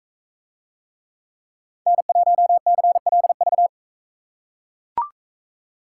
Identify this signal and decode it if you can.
Morse code, heard as a keyed tone, about 42 wpm, 700 Hz. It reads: N1CLV